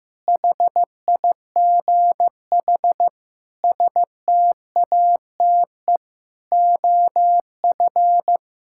Morse code, heard as a keyed tone, about 16 wpm, 700 Hz. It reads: HIGH STATE OF
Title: Sentences from Top 1000 Words - Review Part 6 - 15wpm